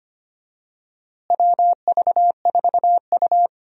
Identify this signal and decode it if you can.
Morse code, heard as a keyed tone, about 25 wpm, 700 Hz. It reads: WV4U
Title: USA Call Signs - 4 Characters - Repeated 2x - 25wpm